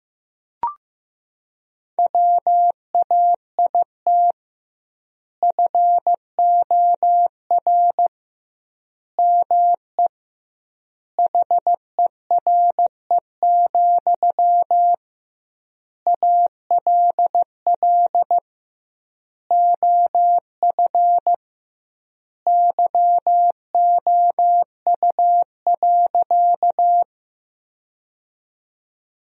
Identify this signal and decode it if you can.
Morse code, heard as a keyed tone, about 15 wpm, 700 Hz. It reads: WAIT FOR ME HERE, ALL OF YOU.
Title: 2x Word-Spacing - Sentences from Top 500 Words - 15wpm